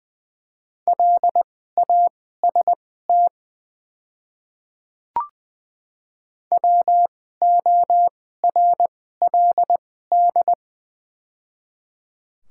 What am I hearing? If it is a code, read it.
Morse code, heard as a keyed tone, about 20 wpm, spaced as for 14 wpm, 700 Hz. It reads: LAST WORLD
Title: 2x Character Spacing - Top 300 Words - 20wpm